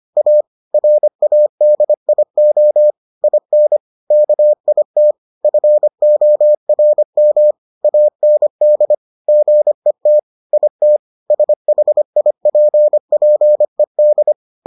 Morse code, heard as a keyed tone, about 25 wpm, 600 Hz. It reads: A RADIO IN KIT FORM AND GET IT SHIPPED